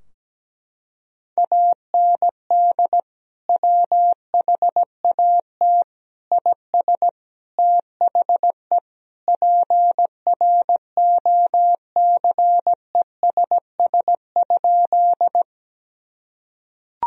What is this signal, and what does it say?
Morse code, heard as a keyed tone, about 17 wpm, 700 Hz. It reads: AND WHAT IS THE PROCESS?
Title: Sentences from Top 1000 Words - Review Part 3 - 17wpm